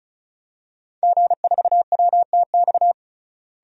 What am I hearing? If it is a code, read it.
Morse code, heard as a keyed tone, about 36 wpm, 700 Hz. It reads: G4WTX
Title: DXCC-100 Call Signs - 5 Characters - Repeated 3x - 35wpm